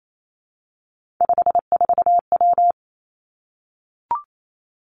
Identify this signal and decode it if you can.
Morse code, heard as a keyed tone, about 28 wpm, 700 Hz. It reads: 54W
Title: Three Challenging Letter-Number Combinations 28wpm